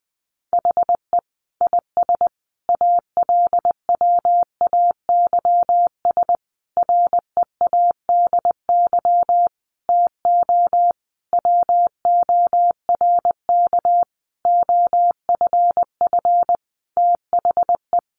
Morse code, heard as a keyed tone, about 20 wpm, 700 Hz. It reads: HE IS ALWAYS READY TO WORK OFF THE
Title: Sentences from Top 1300 Words 20wpm